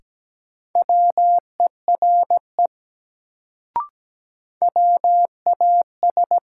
Morse code, heard as a keyed tone, about 17 wpm, 700 Hz. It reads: WERE WAS
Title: Sentences from Top 100 Words - New Word Drill - 17wpm